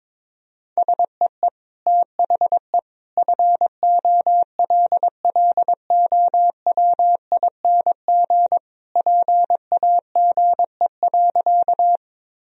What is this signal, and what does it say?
Morse code, heard as a keyed tone, about 22 wpm, 700 Hz. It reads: SEE THE FOLLOWING PAGE.
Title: Sentences from Top 1000 Words - Review Part 1 - 22wpm